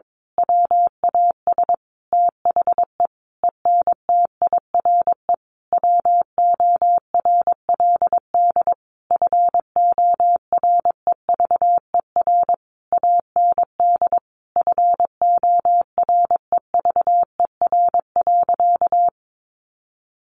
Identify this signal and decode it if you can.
Morse code, heard as a keyed tone, about 22 wpm, 700 Hz. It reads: WAS THE ENTIRE WORLD FOREVER AND FOREVER.
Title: A Sound Of Thunder 22wpm